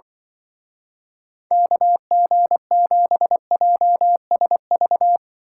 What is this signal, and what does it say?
Morse code, heard as a keyed tone, about 25 wpm, 700 Hz. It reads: KG7JSV